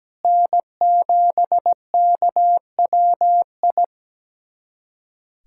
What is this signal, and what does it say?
Morse code, heard as a keyed tone, about 17 wpm, 700 Hz. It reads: N7KWI
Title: USA Callsign DE Callsign 17wpm